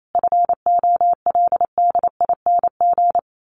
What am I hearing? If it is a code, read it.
Morse code, heard as a keyed tone, about 28 wpm, 700 Hz. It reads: FOLDING
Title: A Sound Of Thunder 28wpm